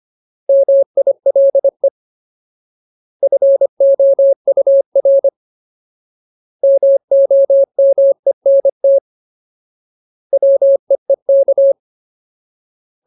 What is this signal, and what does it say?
Morse code, heard as a keyed tone, about 25 wpm, 550 Hz. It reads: MILE FOUR MOMENT WEEK
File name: Ultimate Word List / 25WPM Ultimate